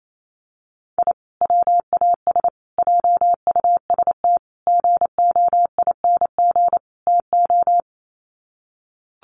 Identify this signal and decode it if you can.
Morse code, heard as a keyed tone, about 28 wpm, 700 Hz. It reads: I WAS JUST GOING TO
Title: Sets of 5 Words Encore 28wpm